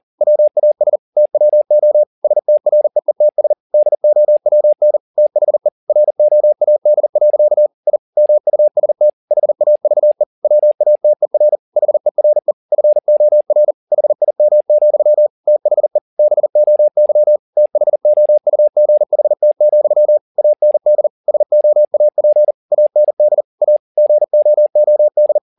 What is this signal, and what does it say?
Morse code, heard as a keyed tone, about 40 wpm, 600 Hz. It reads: WAS TWO STREETS DOWN THE ROAD. I MUST HAVE WATER HERE FOR HIM, THE BOY THOUGHT, AND SOAP AND A GOOD